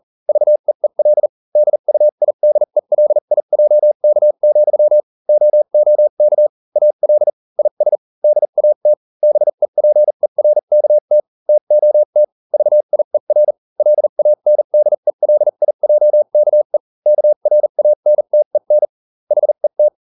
Morse code, heard as a keyed tone, about 40 wpm, spaced as for 35 wpm, 600 Hz. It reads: VEEL DUIDELIJK, OOK AL IS DAT BEPERKT TOT VIER LANDELIJKE KRANTEN HET